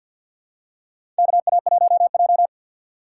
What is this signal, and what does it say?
Morse code, heard as a keyed tone, about 52 wpm, 700 Hz. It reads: KA1J
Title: USA Call Signs - 3x Character Spacing - 50wpm